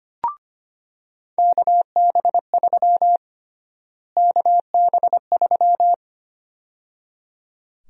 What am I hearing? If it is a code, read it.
Morse code, heard as a keyed tone, about 25 wpm, 700 Hz. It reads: KB3 KB3
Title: USA Call Sign Prefixes - Repeated 2x - 25wpm